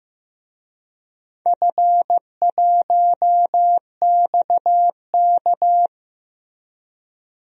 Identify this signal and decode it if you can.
Morse code, heard as a keyed tone, about 15 wpm, 700 Hz. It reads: F1XK
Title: DXCC-100 Call Signs - 4 Characters - Repeated 2x - 15wpm